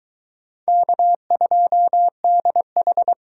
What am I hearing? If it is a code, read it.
Morse code, heard as a keyed tone, about 23 wpm, 700 Hz. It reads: K2DH